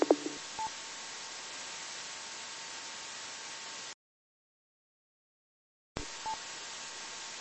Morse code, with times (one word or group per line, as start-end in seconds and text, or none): none